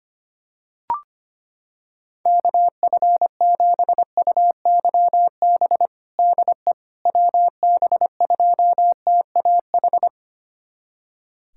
Morse code, heard as a keyed tone, about 25 wpm, 700 Hz. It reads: KF7UYB DE WB2TAH
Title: USA Callsign DE Callsign - 6 Characters - 25wpm